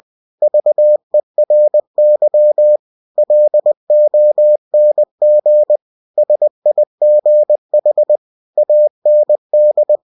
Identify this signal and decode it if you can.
Morse code, heard as a keyed tone, about 20 wpm, 600 Hz. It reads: VERY LONG SIGH AND
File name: cw-books / winnie-the-pooh / winnie-20wpm-page-0093